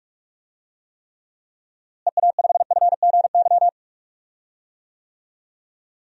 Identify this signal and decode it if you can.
Morse code, heard as a keyed tone, about 46 wpm, 700 Hz. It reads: EA5FGY
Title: DXCC-100 Call Signs - 6 Characters - Repeated 2x - 45wpm